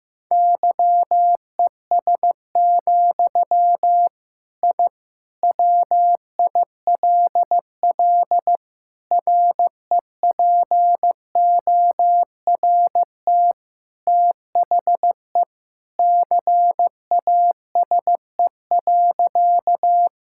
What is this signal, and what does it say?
Morse code, heard as a keyed tone, about 16 wpm, 700 Hz. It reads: YES, I WILL REPORT THE CASE.
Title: Sentences from Top 1000 Words - Review Part 6 - 15wpm